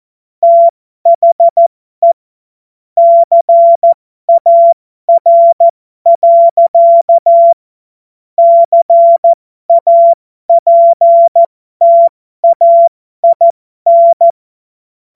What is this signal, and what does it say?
Morse code, heard as a keyed tone, about 14 wpm, spaced as for 12 wpm, 680 Hz. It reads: THE CAR. CAPTAIN